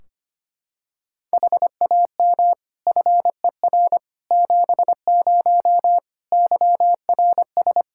Morse code, heard as a keyed tone, about 25 wpm, 700 Hz. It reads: HAM FER 70 YRS
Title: Practice QSO - Exchange 3 - More Info - 25wpm